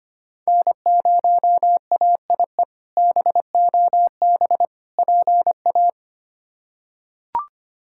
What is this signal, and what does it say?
Morse code, heard as a keyed tone, about 25 wpm, 700 Hz. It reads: N0AIE BOB PA
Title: K1USN SST QSO Practice - 25wpm